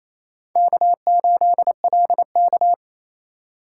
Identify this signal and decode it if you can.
Morse code, heard as a keyed tone, about 29 wpm, 700 Hz. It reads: K8LK
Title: USA Call Signs - 4 Characters - 2x Character Spacing - 28wpm